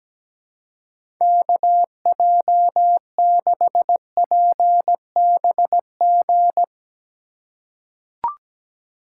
Morse code, heard as a keyed tone, about 17 wpm, 700 Hz. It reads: KJ6PBG
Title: USA Call Signs - 6 Characters - 3x Character Spacing - 17wpm